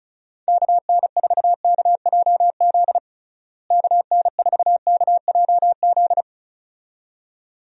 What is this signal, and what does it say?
Morse code, heard as a keyed tone, about 34 wpm, 700 Hz. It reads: KN4KJZ KN4KJZ
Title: USA Call Signs - 6 Characters - Repeated 2x - 35wpm